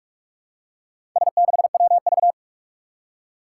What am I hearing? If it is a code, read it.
Morse code, heard as a keyed tone, about 44 wpm, 700 Hz. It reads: S6WV
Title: Four Challenging Letter-Number Combinations 45wpm